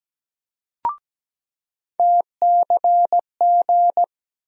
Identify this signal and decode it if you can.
Morse code, heard as a keyed tone, about 17 wpm, 700 Hz. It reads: TCG
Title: USA Call Sign Postfixes - 3 Letters 17wpm